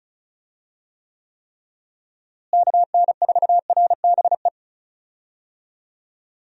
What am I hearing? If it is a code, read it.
Morse code, heard as a keyed tone, about 34 wpm, 700 Hz. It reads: KN4RBE
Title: USA Call Signs - 6 Characters - Repeated 3x - 35wpm